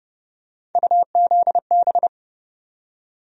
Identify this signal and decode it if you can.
Morse code, heard as a keyed tone, about 31 wpm, 700 Hz. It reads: UZB
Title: USA Call Sign Postfixes - 3 Letters 30wpm